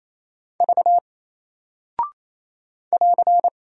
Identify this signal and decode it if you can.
Morse code, heard as a keyed tone, about 28 wpm, 700 Hz. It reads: V <AR>
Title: Single Character (Letter:Number:Punctuation:Prosign) Alt 28wpm